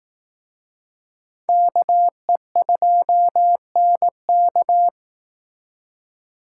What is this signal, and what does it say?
Morse code, heard as a keyed tone, about 18 wpm, 700 Hz. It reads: KE2NK
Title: Speed Racing - USA 2x2 Call Signs 15wpm